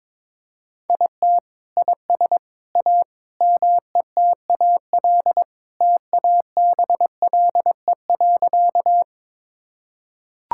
Morse code, heard as a keyed tone, about 22 wpm, 700 Hz. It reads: IT IS A METAL TABLE.
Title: Sentences from Top 1000 Words - Review Part 6 - 22wpm